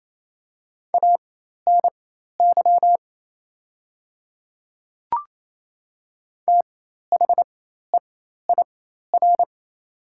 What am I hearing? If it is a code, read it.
Morse code, heard as a keyed tone, about 27 wpm, spaced as for 13 wpm, 700 Hz. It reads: ANY THEIR
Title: 4x Character Spacing - Top 300 Words - 28wpm